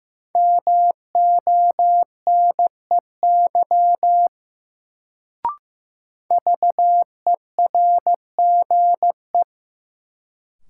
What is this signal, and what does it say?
Morse code, heard as a keyed tone, about 15 wpm, 700 Hz. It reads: MONEY VERGE
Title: Sets of 4 Words - New Word Drill - 15wpm